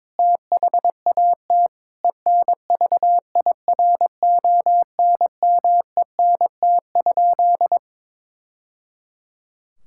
Morse code, heard as a keyed tone, about 21 wpm, 700 Hz. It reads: THAT ENVIRONMENT?
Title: Sentences from Top 1000 Words - Review Part 5 - 22wpm